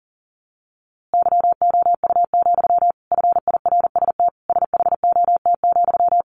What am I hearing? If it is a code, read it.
Morse code, heard as a keyed tone, about 40 wpm, 700 Hz. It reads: YOU, FIRST SHOT,
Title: A Sound Of Thunder 40wpm